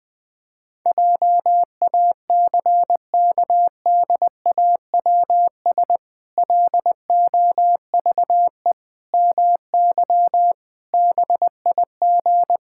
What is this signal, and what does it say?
Morse code, heard as a keyed tone, about 20 wpm, 700 Hz. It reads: JACKDAWS LOVE MY BIG
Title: Pangram Challenge v2 - 20wpm